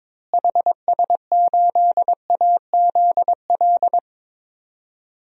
Morse code, heard as a keyed tone, about 22 wpm, 700 Hz. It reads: HS8AZL
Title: DXCC-100 Call Signs - 6 Characters - Repeated 2x - 22wpm